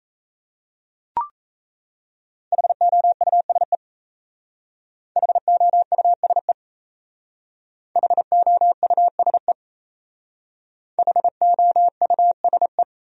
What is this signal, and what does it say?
Morse code, heard as a keyed tone, about 35 wpm, 700 Hz. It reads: HOUSE HOUSE HOUSE HOUSE